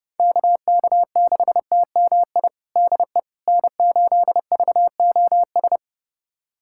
Morse code, heard as a keyed tone, about 30 wpm, 700 Hz. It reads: KK6TMI DE N8VOS
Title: USA Callsign DE Callsign 30wpm